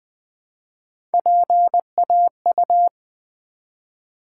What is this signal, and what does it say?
Morse code, heard as a keyed tone, about 20 wpm, 700 Hz. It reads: PAU
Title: Build A Word - 1 Letter At A Time - 5-Letter Words - 20wpm